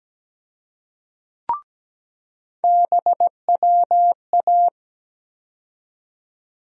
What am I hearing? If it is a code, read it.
Morse code, heard as a keyed tone, about 17 wpm, 700 Hz. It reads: BWA
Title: USA Call Sign Postfixes - 3 Letters 17wpm